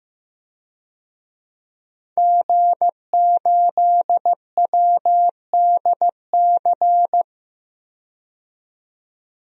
Morse code, heard as a keyed tone, about 15 wpm, 700 Hz. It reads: G8WDC
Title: DXCC-100 Call Signs - 5 Characters - 15wpm